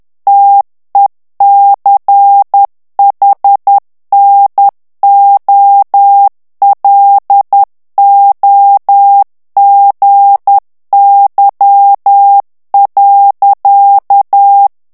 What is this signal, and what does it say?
Morse code, heard as a keyed tone, about 10 wpm, 800 Hz. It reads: TECHNOLOGY.
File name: ANGLAIS texte clair 3 vitesse 10 mots minute